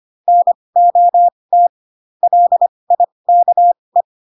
Morse code, heard as a keyed tone, about 25 wpm, spaced as for 20 wpm, 700 Hz. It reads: NOT LIKE
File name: Pinocchio / Pinocchio ch4 15 25 ik1hge